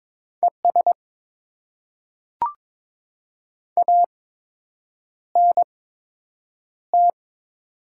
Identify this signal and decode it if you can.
Morse code, heard as a keyed tone, about 23 wpm, spaced as for 8 wpm, 700 Hz. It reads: ES ANT
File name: Top 20 QSO Elements - 8x Character Spacing - 22wpm